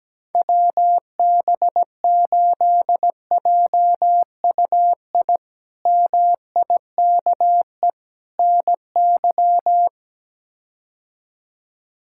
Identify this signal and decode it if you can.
Morse code, heard as a keyed tone, about 17 wpm, 700 Hz. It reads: WB8JUI MIKE NY
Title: K1USN SST QSO Practice - 17wpm